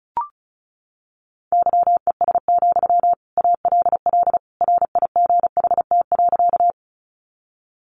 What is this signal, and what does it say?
Morse code, heard as a keyed tone, about 34 wpm, 700 Hz. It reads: YES, ALL RIGHT.
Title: Sentences from Top 500 Words 35wpm